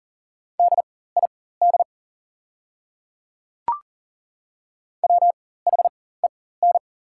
Morse code, heard as a keyed tone, about 37 wpm, spaced as for 18 wpm, 690 Hz. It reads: DID WHEN
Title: 4x Character Spacing - Top 100 Words - 40wpm